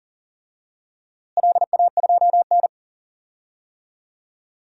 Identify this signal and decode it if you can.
Morse code, heard as a keyed tone, about 39 wpm, 700 Hz. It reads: LA2N
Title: DXCC-100 Call Signs - 4 Characters - Repeated 2x - 40wpm